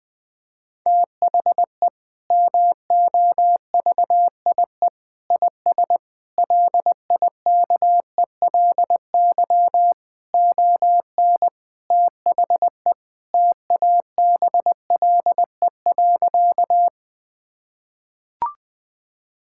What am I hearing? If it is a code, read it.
Morse code, heard as a keyed tone, about 21 wpm, 700 Hz. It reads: THE MOVIE IS LIKELY ON THE TABLE.
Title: Sentences from Top 1000 Words - Review Part 5 - 20wpm